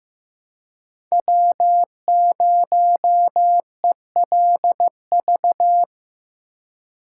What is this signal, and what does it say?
Morse code, heard as a keyed tone, about 15 wpm, 700 Hz. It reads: W0ELV
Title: Code Groups - 5 Letter:Numbers - 15wpm